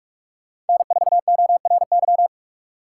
Morse code, heard as a keyed tone, about 48 wpm, 700 Hz. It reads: N4ORY
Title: Speed Racing - USA 1x3 Call Signs 45wpm